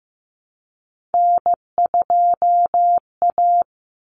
Morse code, heard as a keyed tone, about 14 wpm, 700 Hz. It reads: N2A
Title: Common Cut Numbers - 3-Digit 15wpm